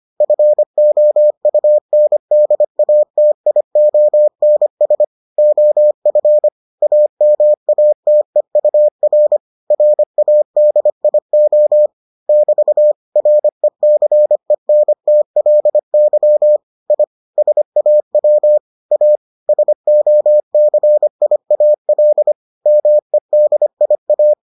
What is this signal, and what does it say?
Morse code, heard as a keyed tone, about 25 wpm, 600 Hz. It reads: FOUNDATIONS OF AMATEUR RADIO <BT> RECENTLY I SAW A SOCIAL MEDIA